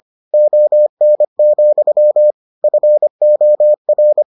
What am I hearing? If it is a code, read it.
Morse code, heard as a keyed tone, about 25 wpm, 600 Hz. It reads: ON, FOR